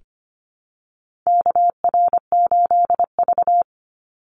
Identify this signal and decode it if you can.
Morse code, heard as a keyed tone, about 25 wpm, 700 Hz. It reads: KR8V
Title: Call Signs - 25wpm